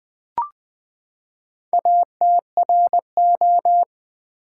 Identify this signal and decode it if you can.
Morse code, heard as a keyed tone, about 20 wpm, 700 Hz. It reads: ATRO
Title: Build A Word - 4 Letters At A Time - 8-Letter Words - 20wpm